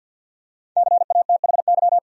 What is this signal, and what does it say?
Morse code, heard as a keyed tone, about 52 wpm, 700 Hz. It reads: CATHY
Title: USA Common Names - Max 5 Letters - 50wpm